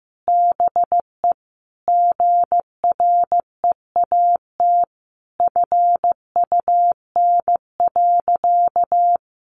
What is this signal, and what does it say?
Morse code, heard as a keyed tone, about 14 wpm, 700 Hz. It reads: BE GREAT FUN.
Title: Sentences from Top 700 Words 15wpm